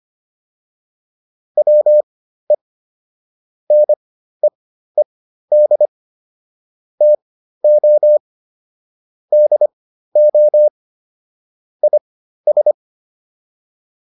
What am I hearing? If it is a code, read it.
Morse code, heard as a keyed tone, about 25 wpm, spaced as for 13 wpm, 600 Hz. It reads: WE NEED TO DO IS